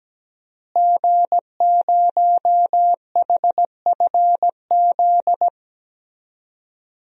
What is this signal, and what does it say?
Morse code, heard as a keyed tone, about 17 wpm, 700 Hz. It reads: G0HFZ
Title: DXCC-100 Call Signs - 5 Characters - Repeated 3x - 17wpm